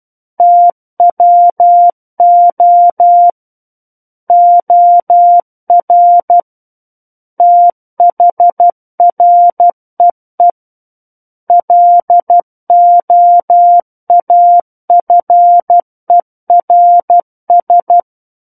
Morse code, heard as a keyed tone, about 12 wpm, 700 Hz. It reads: TWO OR THREE LOAFERS
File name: Book One / 03-Chapter 3 - 12 WPM